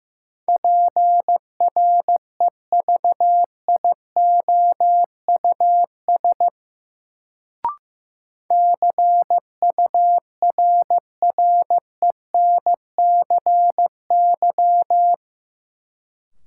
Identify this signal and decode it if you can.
Morse code, heard as a keyed tone, about 15 wpm, 700 Hz. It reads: PREVIOUS CURRENCY
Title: Sentences from Top 1500 Words - New Word Drill - 15wpm